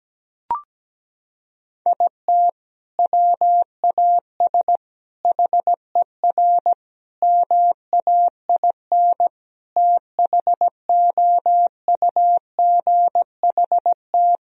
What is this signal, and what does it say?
Morse code, heard as a keyed tone, about 17 wpm, 700 Hz. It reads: IT WAS HER MAIN THOUGHT
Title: Sentences from Top 1000 Words - Review Part 2 - 17wpm